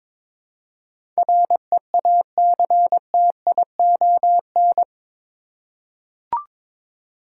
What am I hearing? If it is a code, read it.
Morse code, heard as a keyed tone, about 22 wpm, 700 Hz. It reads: REACTION
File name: practice / 22wpm / Eight Letter Words - 22wpm